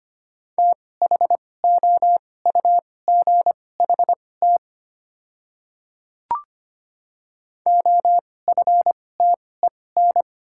Morse code, heard as a keyed tone, about 25 wpm, spaced as for 18 wpm, 700 Hz. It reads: THOUGHT OFTEN
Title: 2x Character Spacing - Top 300 Words - 25wpm